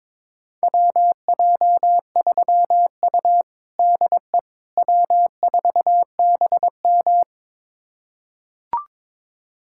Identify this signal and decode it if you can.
Morse code, heard as a keyed tone, about 22 wpm, 700 Hz. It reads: WJ3U DE W4BM
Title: USA Callsign DE Callsign - 4 Characters - 22wpm